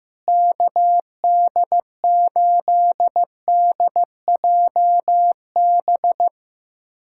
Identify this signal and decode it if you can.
Morse code, heard as a keyed tone, about 15 wpm, 700 Hz. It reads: KD8DJB
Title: USA Callsign DE Callsign 15wpm